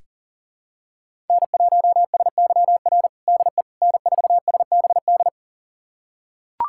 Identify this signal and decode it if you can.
Morse code, heard as a keyed tone, about 40 wpm, 700 Hz. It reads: N1SYR DE N4SBD
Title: USA Callsign DE Callsign - 5 Characters - 40wpm